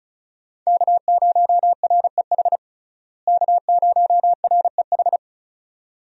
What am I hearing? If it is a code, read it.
Morse code, heard as a keyed tone, about 36 wpm, 700 Hz. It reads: K0REH K0REH
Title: USA Call Signs - 5 Characters - Repeated 2x - 35wpm